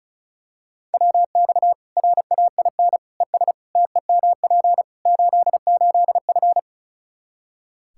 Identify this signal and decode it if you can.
Morse code, heard as a keyed tone, about 36 wpm, 700 Hz. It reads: WX RAIN ES TEMP 88F
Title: Practice QSO - Exchange 2 - Info - 35wpm